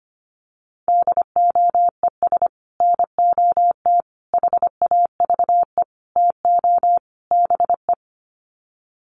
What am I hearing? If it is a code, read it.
Morse code, heard as a keyed tone, about 25 wpm, 700 Hz. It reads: DOES NOT HAVE TO BE
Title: Sets of 5 Words 25wpm